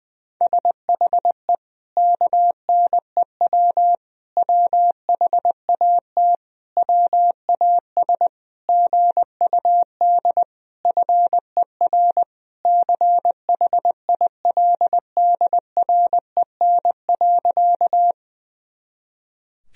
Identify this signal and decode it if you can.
Morse code, heard as a keyed tone, about 20 wpm, 700 Hz. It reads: SHE KNEW WHAT WAS GUD FER CHILDREN.
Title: Sentences from Top 400 Words and Abbreviations 20wpm